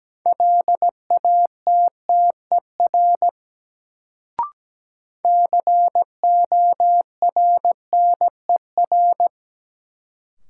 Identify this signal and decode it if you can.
Morse code, heard as a keyed tone, about 17 wpm, 700 Hz. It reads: LATTER CORNER
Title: Sentences from Top 1600 Words - New Word Drill Alt - 17wpm